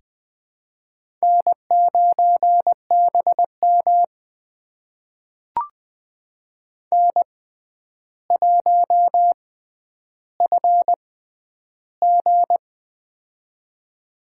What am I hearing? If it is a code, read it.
Morse code, heard as a keyed tone, about 20 wpm, spaced as for 10 wpm, 700 Hz. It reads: N9BM N1FG
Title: USA Call Signs - 4 Characters - 6x Character Spacing - 20wpm